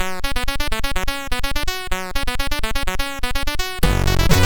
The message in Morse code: none